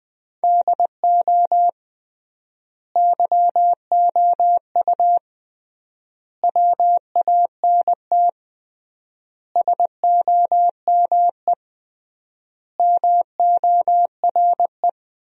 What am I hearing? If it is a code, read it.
Morse code, heard as a keyed tone, about 20 wpm, 700 Hz. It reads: DO YOU WANT SOME MORE
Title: 3x Word-Spacing - Sentences from Top 300 Words - 20wpm